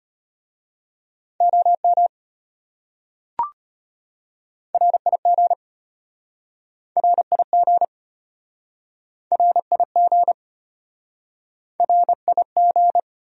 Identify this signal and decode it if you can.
Morse code, heard as a keyed tone, about 33 wpm, 700 Hz. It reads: OM RIG RIG RIG RIG